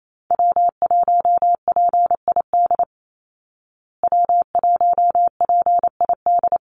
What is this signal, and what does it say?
Morse code, heard as a keyed tone, about 28 wpm, 700 Hz. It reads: W1PID W1PID
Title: Call Signs Repeated 3x - 28wpm